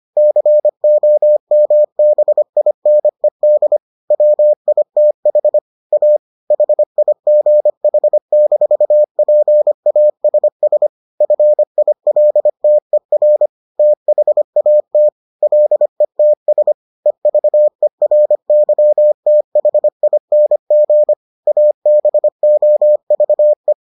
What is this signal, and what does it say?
Morse code, heard as a keyed tone, about 25 wpm, 600 Hz. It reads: COMBINED WITH A HIGH-PASS FILTER THAT LETS EVERYTHING ABOVE